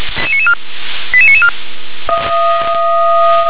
Five-tone selective call (ZVEI), heard as quick runs of five tones, tones of about 70 ms. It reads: 88994